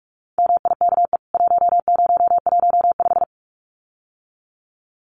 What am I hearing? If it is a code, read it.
Morse code, heard as a keyed tone, about 44 wpm, 700 Hz. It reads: MIKE 1015
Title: CWT Exchange 45wpm